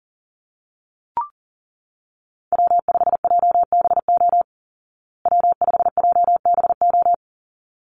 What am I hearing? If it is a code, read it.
Morse code, heard as a keyed tone, about 40 wpm, 700 Hz. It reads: W5JBO W5JBO
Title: Call Signs Repeated 3x - 40wpm